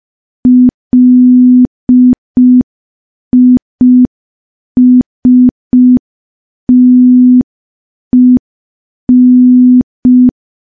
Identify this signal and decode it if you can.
Morse code, heard as a keyed tone, about 5 wpm, 250 Hz. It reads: LISTEN
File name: morse / morse-training-05WPM